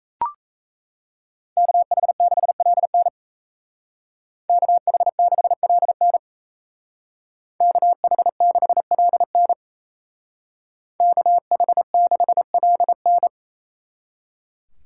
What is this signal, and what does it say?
Morse code, heard as a keyed tone, about 36 wpm, 700 Hz. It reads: KH6LN KH6LN KH6LN KH6LN